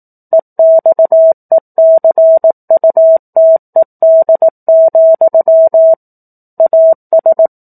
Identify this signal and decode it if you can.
Morse code, heard as a keyed tone, about 18 wpm, 650 Hz. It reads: EXECUTED, AS
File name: 04-Chapter-04 18 WPM